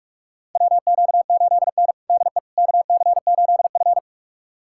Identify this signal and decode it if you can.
Morse code, heard as a keyed tone, about 41 wpm, 700 Hz. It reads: WQ8N DE KC8F
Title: USA Callsign DE Callsign - 4 Characters - 45wpm